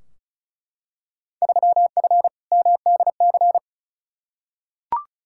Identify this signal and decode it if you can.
Morse code, heard as a keyed tone, about 36 wpm, 690 Hz. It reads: 3F MDC
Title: Field Day Exchanges Encore - 35wpm